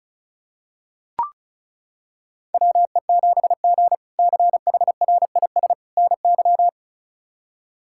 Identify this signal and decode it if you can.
Morse code, heard as a keyed tone, about 34 wpm, 700 Hz. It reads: WE7G CHRIS NY
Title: K1USN SST QSO Practice - 35wpm